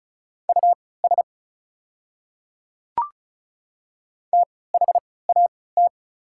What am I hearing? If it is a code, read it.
Morse code, heard as a keyed tone, about 36 wpm, spaced as for 20 wpm, 700 Hz. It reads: US THAT
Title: 3x Character Spacing - Top 200 Words - 35wpm